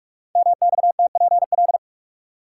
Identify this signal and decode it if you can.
Morse code, heard as a keyed tone, about 44 wpm, 700 Hz. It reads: MXTPL
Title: Code Groups - 5 Letters - 45wpm